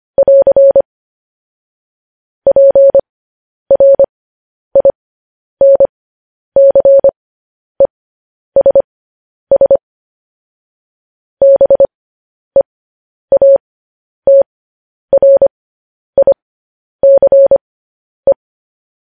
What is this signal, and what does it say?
Morse code, heard as a keyed tone, about 25 wpm, spaced as for 10 wpm, 550 Hz. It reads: <AR> PRINCESS BEATRICE
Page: news-headlines-in-morse-code-at-10-wpm